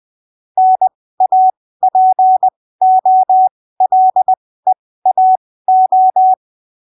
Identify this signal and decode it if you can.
Morse code, heard as a keyed tone, about 20 wpm, spaced as for 15 wpm, 750 Hz. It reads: NAPOLEAO
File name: arquivos / texto / texto 26 20ppm